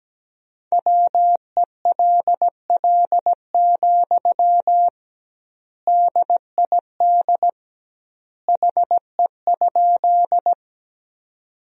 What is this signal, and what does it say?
Morse code, heard as a keyed tone, about 17 wpm, 700 Hz. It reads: WELL, DID HE?
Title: 2x Word-Spacing - Sentences from Top 200 Words - 17wpm